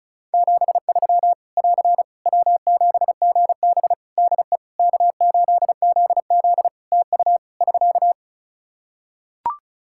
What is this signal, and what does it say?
Morse code, heard as a keyed tone, about 36 wpm, 700 Hz. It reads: 73 <AR> W7GB DE K8ZZ TU <SK>
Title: Practice QSO - Exchange 4 - Ending - 35wpm